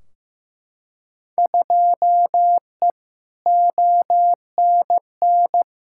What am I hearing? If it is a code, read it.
Morse code, heard as a keyed tone, about 15 wpm, 700 Hz. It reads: 2E ONN
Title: Field Day Exchanges Encore - 15wpm